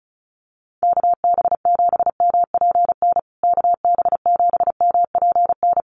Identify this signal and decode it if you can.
Morse code, heard as a keyed tone, about 34 wpm, 700 Hz. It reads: KB7MPN KB7MPN
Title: USA Call Signs Repeated 3x 35wpm